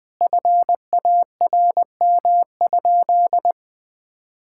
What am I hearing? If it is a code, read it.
Morse code, heard as a keyed tone, about 21 wpm, 700 Hz. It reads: FARM?
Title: Sentences from Top 1000 Words - Review Part 6 - 20wpm